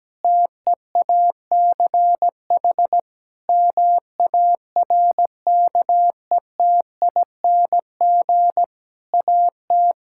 Morse code, heard as a keyed tone, about 17 wpm, 700 Hz. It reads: TEACH MARKETING AT
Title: Sentences from Top 1000 Words - Review Part 5 - 17wpm